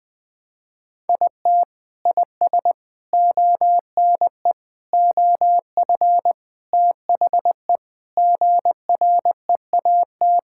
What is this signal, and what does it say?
Morse code, heard as a keyed tone, about 20 wpm, 700 Hz. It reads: IT IS ONE OF THE GREAT
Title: Sentences with 4-Word Phrases - 20wpm